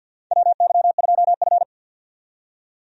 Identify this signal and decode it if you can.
Morse code, heard as a keyed tone, about 47 wpm, 700 Hz. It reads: WY2F
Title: USA Call Signs - 4 Characters - 50wpm